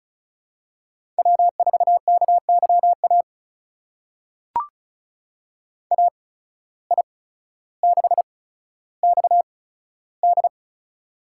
Morse code, heard as a keyed tone, about 36 wpm, spaced as for 15 wpm, 700 Hz. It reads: W4KYA AI6XD
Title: USA Call Signs - 5 Characters - 8x Character Spacing - 35wpm